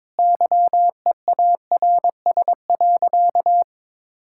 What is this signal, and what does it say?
Morse code, heard as a keyed tone, about 22 wpm, 700 Hz. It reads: YEARS.
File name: practice / 22wpm / Sentences with 3-Word Phrases - 22wpm